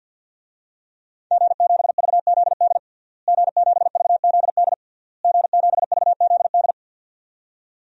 Morse code, heard as a keyed tone, about 47 wpm, 700 Hz. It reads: G7VZD G7VZD G7VZD
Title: DXCC-100 Call Signs - 5 Characters - Repeated 3x - 50wpm